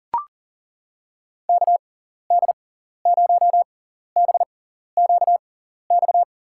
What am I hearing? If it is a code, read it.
Morse code, heard as a keyed tone, about 40 wpm, spaced as for 14 wpm, 700 Hz. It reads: KD0BQX
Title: USA Call Signs - 6 Characters - 6x Character Spacing - 40wpm